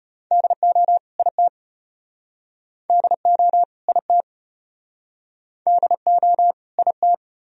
Morse code, heard as a keyed tone, about 34 wpm, 700 Hz. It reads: DO IT DO IT DO IT